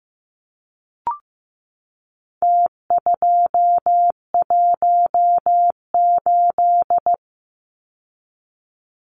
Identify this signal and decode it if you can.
Morse code, heard as a keyed tone, about 15 wpm, 700 Hz. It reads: T218
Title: Numbers - Contest Style - 4-Digit 15wpm